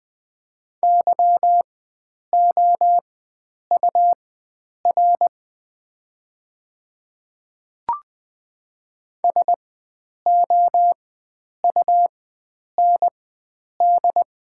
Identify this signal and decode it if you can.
Morse code, heard as a keyed tone, about 20 wpm, spaced as for 9 wpm, 690 Hz. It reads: YOUR SOUND
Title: 4x Character Spacing - Top 100 Words - 20wpm